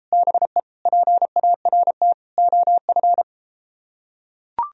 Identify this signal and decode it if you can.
Morse code, heard as a keyed tone, about 31 wpm, 700 Hz. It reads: BE PART OF